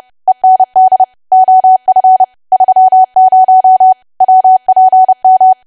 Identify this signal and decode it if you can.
Morse code, heard as a keyed tone, about 30 wpm, 750 Hz. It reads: END OF 30 WPM
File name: cwa / W1AW-Level3 / 30WPM